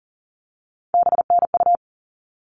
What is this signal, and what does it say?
Morse code, heard as a keyed tone, about 40 wpm, 700 Hz. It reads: BNU
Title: All Cut Numbers - 3-Digit 40wpm